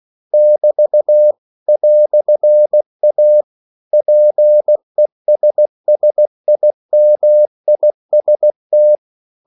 Morse code, heard as a keyed tone, about 16 wpm, 600 Hz. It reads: <BT> "A PESSIMIST
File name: CW / Practice / EnglishQuotations0416